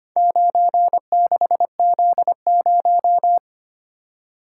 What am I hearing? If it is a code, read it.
Morse code, heard as a keyed tone, about 25 wpm, 700 Hz. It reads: 96Z0
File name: practice / 25wpm / Four Letter-Number Combinations - 25wpm